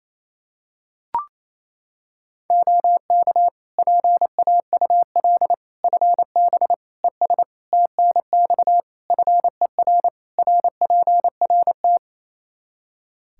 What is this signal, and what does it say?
Morse code, heard as a keyed tone, about 27 wpm, 700 Hz. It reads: OK PAUL FB ES TNX FER RPRT
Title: K1USN SST QSO Practice 28wpm